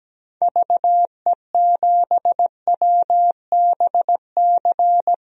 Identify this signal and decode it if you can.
Morse code, heard as a keyed tone, about 17 wpm, 700 Hz. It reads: VE7WBC
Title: DXCC-100 Call Signs - 6 Characters - 17wpm